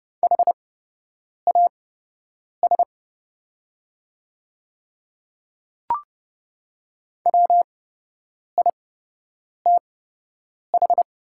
Morse code, heard as a keyed tone, about 29 wpm, spaced as for 8 wpm, 690 Hz. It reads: HAS WITH
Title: 8x Character Spacing - Top 100 Words - 30wpm